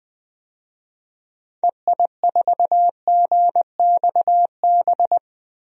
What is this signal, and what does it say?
Morse code, heard as a keyed tone, about 21 wpm, 700 Hz. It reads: EI4GXB
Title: DXCC-100 Call Signs - 6 Characters - 20wpm